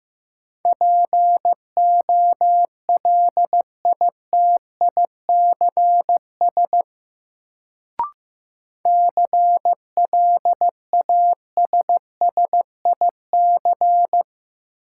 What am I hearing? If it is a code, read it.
Morse code, heard as a keyed tone, about 15 wpm, 700 Hz. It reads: POLITICS CLASSIC
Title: Sentences from Top 1000 Words - New Word Drill - 15wpm